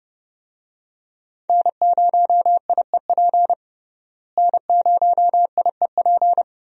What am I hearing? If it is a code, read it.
Morse code, heard as a keyed tone, about 30 wpm, 700 Hz. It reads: N0IEP N0IEP
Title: USA Call Signs - 5 Characters - Repeated 3x - 30wpm